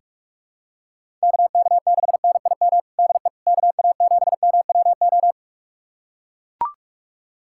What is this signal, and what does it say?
Morse code, heard as a keyed tone, about 43 wpm, 700 Hz. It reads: KK6NIM DE KA7MWO
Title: USA Callsign DE Callsign - 6 Characters - 45wpm